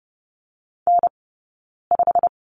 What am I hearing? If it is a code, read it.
Morse code, heard as a keyed tone, about 30 wpm, spaced as for 9 wpm, 690 Hz. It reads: N5
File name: Lesson 17d - Letter u - Two Characters 30wpm